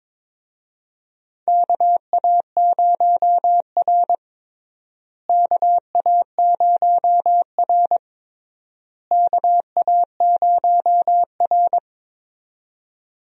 Morse code, heard as a keyed tone, about 22 wpm, 700 Hz. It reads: KA0R KA0R KA0R
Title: USA Call Signs - 4 Characters - Repeated 3x - 22wpm